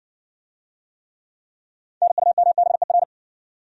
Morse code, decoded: NUMBER

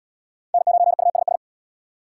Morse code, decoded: N9USS

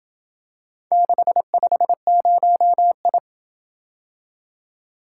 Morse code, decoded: 650I

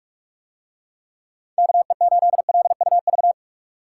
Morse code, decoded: KE8LUV

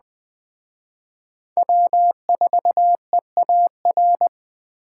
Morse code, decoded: W4EAR

